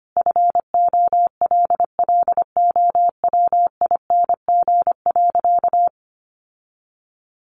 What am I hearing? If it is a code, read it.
Morse code, heard as a keyed tone, about 25 wpm, 700 Hz. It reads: FOLLOWING.
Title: Sentences from Top 1300 Words 25wpm